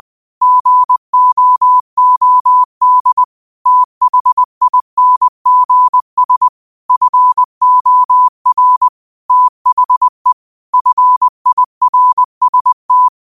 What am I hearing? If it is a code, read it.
Morse code, heard as a keyed tone, about 20 wpm, 1000 Hz. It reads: GOOD THINGS FOR THE FIRST